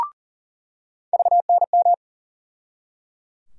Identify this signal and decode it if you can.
Morse code, heard as a keyed tone, about 40 wpm, 700 Hz. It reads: VNM
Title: USA Call Sign Postfixes - 3 Letters 40wpm